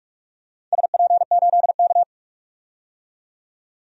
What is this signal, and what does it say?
Morse code, heard as a keyed tone, about 44 wpm, 700 Hz. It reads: SP8K
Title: DXCC-100 Call Signs - 4 Characters - 45wpm